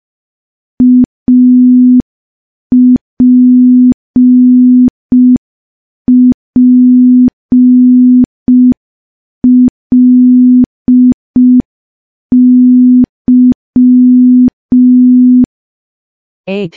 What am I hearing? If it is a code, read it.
Morse code, heard as a keyed tone, about 5 wpm, 250 Hz. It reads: APPLY
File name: morse / morse-training-05WPM